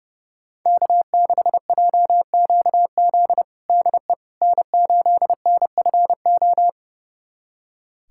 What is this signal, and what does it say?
Morse code, heard as a keyed tone, about 30 wpm, 700 Hz. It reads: K6JQZ DE N8NFO
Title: USA Callsign DE Callsign - 5 Characters - 30wpm